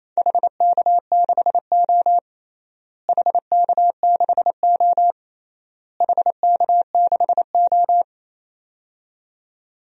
Morse code, heard as a keyed tone, about 28 wpm, 700 Hz. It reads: HK6O HK6O HK6O
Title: DXCC-100 Call Signs - 4 Characters - Repeated 3x - 28wpm